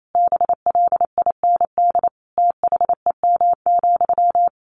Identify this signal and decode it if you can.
Morse code, heard as a keyed tone, about 28 wpm, 700 Hz. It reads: BLIND THEM,
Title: A Sound Of Thunder 28wpm